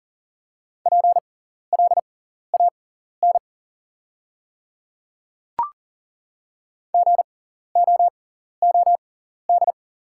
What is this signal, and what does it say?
Morse code, heard as a keyed tone, about 40 wpm, spaced as for 14 wpm, 690 Hz. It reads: PLAN GOOD
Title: 6x Character Spacing - Top 500 Words - 40wpm